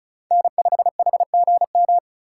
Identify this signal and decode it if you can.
Morse code, heard as a keyed tone, about 36 wpm, 700 Hz. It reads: N5HGM
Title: USA Call Signs - 5 Characters - 3x Character Spacing - 35wpm